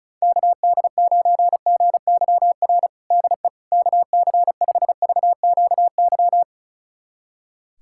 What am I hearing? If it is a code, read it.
Morse code, heard as a keyed tone, about 36 wpm, 700 Hz. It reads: KD9GYR DE KC5VQY
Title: USA Callsign DE Callsign - 6 Characters - 35wpm